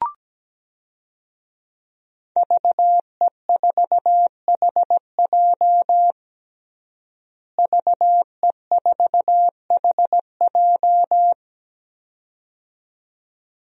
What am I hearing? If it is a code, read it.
Morse code, heard as a keyed tone, about 17 wpm, 700 Hz. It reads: VE4HJ VE4HJ
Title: DXCC-100 Call Signs - 5 Characters - Repeated 2x - 17wpm